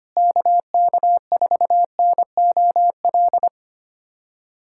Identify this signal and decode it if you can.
Morse code, heard as a keyed tone, about 25 wpm, 700 Hz. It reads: KK4NOL
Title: USA Callsign DE Callsign 25wpm